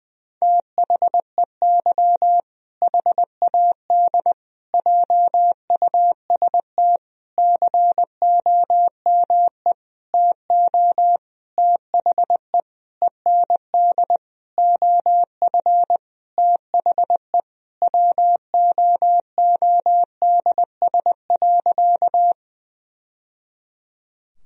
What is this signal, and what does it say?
Morse code, heard as a keyed tone, about 20 wpm, 700 Hz. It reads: THEY HAD JUST COME TO THE END OF THE WOODS.
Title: Sentences from Top 1000 Words - Review Part 2 - 20wpm